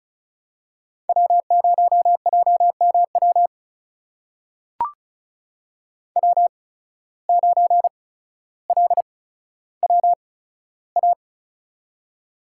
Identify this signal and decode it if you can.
Morse code, heard as a keyed tone, about 36 wpm, spaced as for 15 wpm, 700 Hz. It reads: W0JMW W9LWA
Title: USA Call Signs - 8x Character Spacing - 35wpm